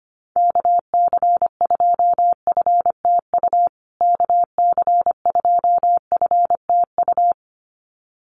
Morse code, heard as a keyed tone, about 25 wpm, 700 Hz. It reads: KC2FTU KC2FTU
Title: USA Call Signs Repeated 2x 25wpm